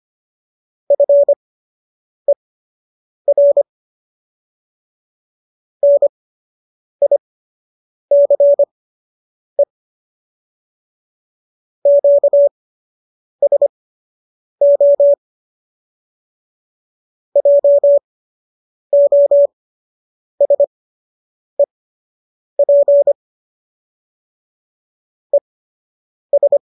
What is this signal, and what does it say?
Morse code, heard as a keyed tone, about 25 wpm, spaced as for 8 wpm, 570 Hz. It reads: FER NICE QSO JOSEP ES